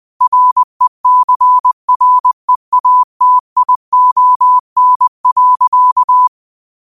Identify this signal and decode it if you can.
Morse code, heard as a keyed tone, about 20 wpm, 1000 Hz. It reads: RECREATION.
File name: qotdfiles / 20wpm